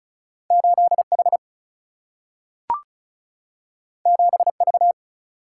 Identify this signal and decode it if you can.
Morse code, heard as a keyed tone, about 36 wpm, 700 Hz. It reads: 8 H 7 V
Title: Single-Letter Single-Number 35wpm